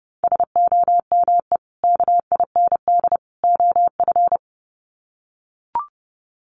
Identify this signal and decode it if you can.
Morse code, heard as a keyed tone, about 30 wpm, 700 Hz. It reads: SOME KIND OF
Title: Sets of 3 Words 30wpm